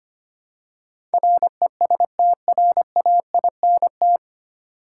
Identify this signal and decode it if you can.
Morse code, heard as a keyed tone, about 25 wpm, 700 Hz. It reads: RESTRAINT
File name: practice / 25wpm / 9 Letter Words - First 3 Letters Given - 25wpm